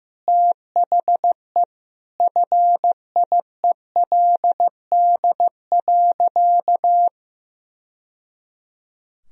Morse code, heard as a keyed tone, about 14 wpm, 700 Hz. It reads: THE FIELD.
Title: Sentences from Top 1000 Words - Review Part 4 - 15wpm